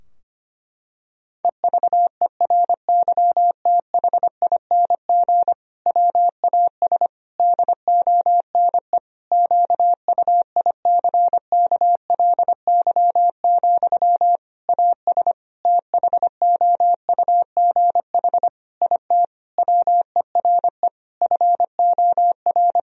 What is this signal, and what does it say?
Morse code, heard as a keyed tone, about 25 wpm, 700 Hz. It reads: EVERYTHING WAS DONE QUICKLY, AS THOUGH IT WERE FOR
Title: Sentences from Top 1000 Words - Review Part 2 - 25wpm